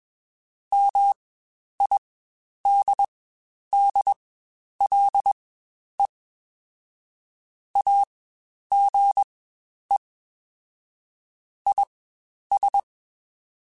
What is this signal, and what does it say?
Morse code, heard as a keyed tone, about 21 wpm, spaced as for 10 wpm, 800 Hz. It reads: MIDDLE AGE IS
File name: Spaced / Chuckles Spaced-mp3s / MiddleAge sp @20